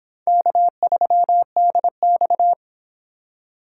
Morse code, heard as a keyed tone, about 24 wpm, 700 Hz. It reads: K3DX